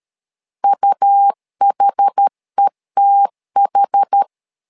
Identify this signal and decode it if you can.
Morse code, heard as a keyed tone, about 13 wpm, 800 Hz. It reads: UHETH
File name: LEZIONI / 2 UHDN / 2 UH 12wpm 1439